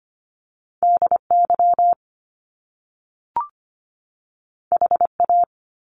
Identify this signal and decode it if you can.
Morse code, heard as a keyed tone, about 25 wpm, 700 Hz. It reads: DY HA
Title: Top Two Letter Combinations in Words 25wpm